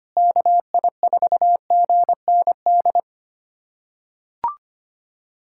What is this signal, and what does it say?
Morse code, heard as a keyed tone, about 25 wpm, 700 Hz. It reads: KI4GND
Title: USA Callsign DE Callsign 25wpm